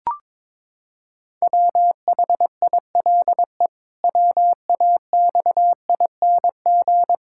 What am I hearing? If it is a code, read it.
Morse code, heard as a keyed tone, about 22 wpm, 700 Hz. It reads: WHILE WAXING
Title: Pangram Challenge v2 - 22wpm